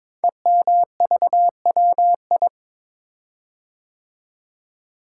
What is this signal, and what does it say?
Morse code, heard as a keyed tone, about 22 wpm, 690 Hz. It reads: EMVWI
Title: Code Groups - 5 Letters - 22wpm